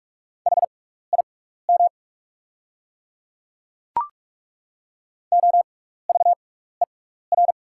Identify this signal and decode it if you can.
Morse code, heard as a keyed tone, about 43 wpm, spaced as for 15 wpm, 690 Hz. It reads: HIM OVER
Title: 6x Character Spacing - Top 100 Words - 45wpm